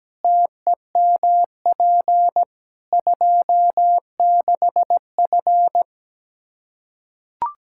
Morse code, heard as a keyed tone, about 17 wpm, 700 Hz. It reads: TEMP 26F
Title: Practice QSO - Exchange 2 - Info - 17wpm